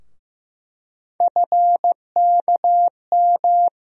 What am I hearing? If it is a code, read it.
Morse code, heard as a keyed tone, about 15 wpm, 700 Hz. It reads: FKM
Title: USA Call Sign Postfixes - 3 Letters 15wpm